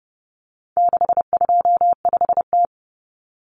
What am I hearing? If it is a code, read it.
Morse code, heard as a keyed tone, about 30 wpm, 700 Hz. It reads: 625T
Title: Common Cut Numbers - 4-Digit 30wpm